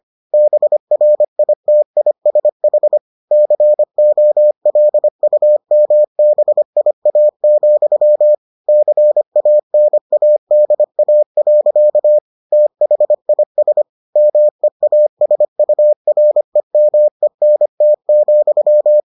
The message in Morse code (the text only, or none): BRITISH COLUMBIA, CANADA. THIS MEASUREMENT,